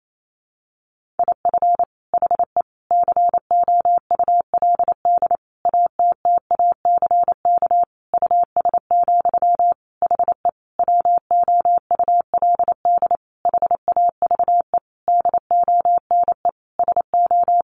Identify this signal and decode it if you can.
Morse code, heard as a keyed tone, about 28 wpm, 700 Hz. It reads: IF HE COULD ATTACK US, HE WOULD HAVE DONE SO
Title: Sentences from Top 1500 Words 28wpm